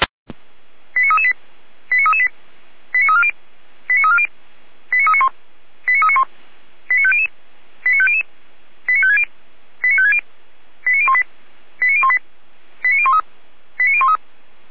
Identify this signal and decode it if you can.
Five-tone selective call (ZVEI), heard as quick runs of five tones, tones of about 70 ms. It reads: 89338 89338 89340 89340 89391 89391 89500 89500 89560 89560 89018 89018 89013 89013